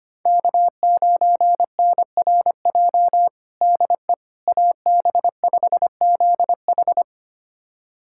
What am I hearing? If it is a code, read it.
Morse code, heard as a keyed tone, about 25 wpm, 700 Hz. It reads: K9NRJ DE AB5ZH
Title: USA Callsign DE Callsign - 5 Characters - 25wpm